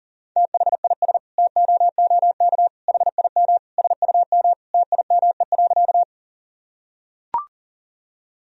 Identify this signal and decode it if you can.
Morse code, heard as a keyed tone, about 40 wpm, 700 Hz. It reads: THIS TOOK HIM SUM TIME.
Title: Sentences from Top 300 Words and Abbreviations 40wpm